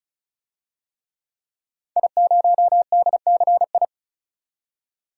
Morse code, heard as a keyed tone, about 34 wpm, 700 Hz. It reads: I0DCI
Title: DXCC-100 Call Signs - 5 Characters - 35wpm